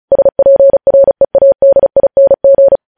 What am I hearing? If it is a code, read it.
Morse code, heard as a keyed tone, about 36 wpm, 550 Hz. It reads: SPREADING
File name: podcast / News headlines-1x-35x35